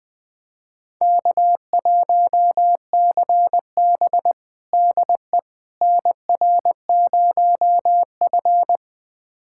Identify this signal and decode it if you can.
Morse code, heard as a keyed tone, about 20 wpm, 700 Hz. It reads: K1CB DE NR0F
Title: USA Callsign DE Callsign - 4 Characters - 20wpm